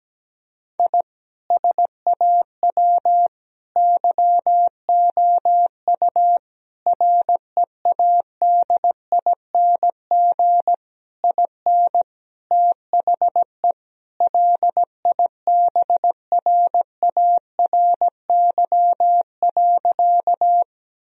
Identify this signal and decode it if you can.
Morse code, heard as a keyed tone, about 17 wpm, 700 Hz. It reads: I SAW YOU READING IN THE LIBRARY.
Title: Sentences from Top 1000 Words - Review Part 4 - 17wpm